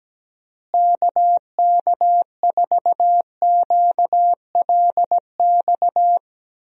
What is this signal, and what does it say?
Morse code, heard as a keyed tone, about 17 wpm, 700 Hz. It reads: KK4QLX
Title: USA Call Signs - 6 Characters - Repeated 3x - 17wpm